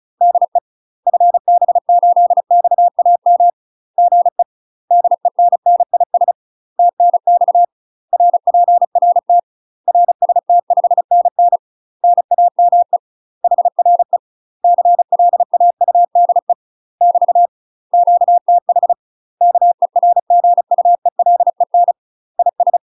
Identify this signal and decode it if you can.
Morse code, heard as a keyed tone, about 35 wpm, 700 Hz. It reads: DE FB8XAM GE DENNIS TNX RPRT RST5NN NAME HRE CLAUDE <BT> QTH KERGUELEN IS
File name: UK MASTER QSOs TRAINING / 35WPM / UK MASTER QSOs 0016